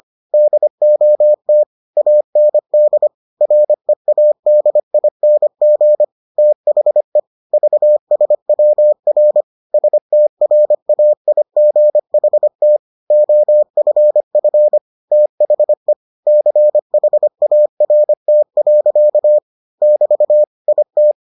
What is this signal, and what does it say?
Morse code, heard as a keyed tone, about 25 wpm, 600 Hz. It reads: DOT AND READING THE VSWR STRAIGHT OFF THE CHART. <BT> IT